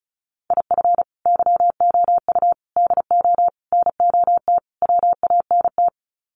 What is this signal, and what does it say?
Morse code, heard as a keyed tone, about 34 wpm, 700 Hz. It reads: IF YOU DO NOT WANT
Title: Sets of 5 Words 35wpm